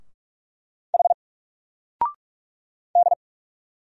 Morse code, heard as a keyed tone, about 46 wpm, 700 Hz. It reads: H D
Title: Single Letter-Number 45wpm